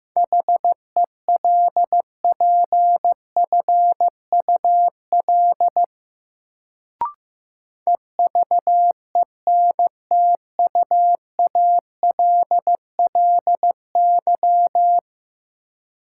Morse code, heard as a keyed tone, about 15 wpm, 700 Hz. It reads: HELPFUL EVENTUALLY
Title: Sentences from Top 1000 Words - Review Part 6 - New Word Drill - 15wpm